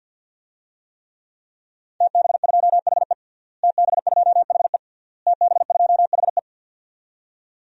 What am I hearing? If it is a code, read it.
Morse code, heard as a keyed tone, about 53 wpm, 700 Hz. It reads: TB2HE TB2HE TB2HE
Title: DXCC-100 Call Signs - 5 Characters - Repeated 3x - 50wpm